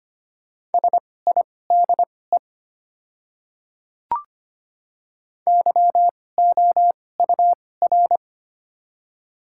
Morse code, heard as a keyed tone, about 25 wpm, spaced as for 18 wpm, 700 Hz. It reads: SIDE YOUR
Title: 2x Character Spacing - Top 200 Words - 25wpm